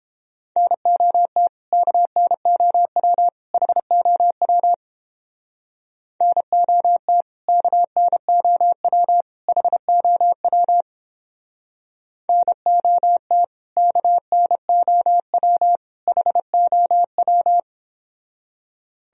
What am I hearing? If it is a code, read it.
Morse code, heard as a keyed tone, about 29 wpm, 700 Hz. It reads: NOT KNOW HOW NOT KNOW HOW NOT KNOW HOW